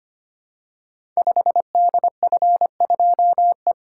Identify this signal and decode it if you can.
Morse code, heard as a keyed tone, about 25 wpm, 700 Hz. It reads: 5DF2E
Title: Code Groups - 5 Letter:Numbers - 25wpm